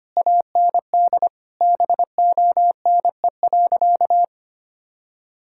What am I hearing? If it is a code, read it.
Morse code, heard as a keyed tone, about 25 wpm, 700 Hz. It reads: AND BONE.
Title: Sentences from Top 1600 Words 25wpm